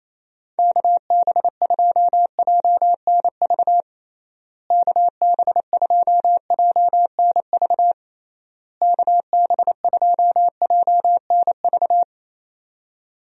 Morse code, heard as a keyed tone, about 28 wpm, 700 Hz. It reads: KB2JNV KB2JNV KB2JNV
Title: USA Call Signs - 6 Characters - Repeated 3x - 28wpm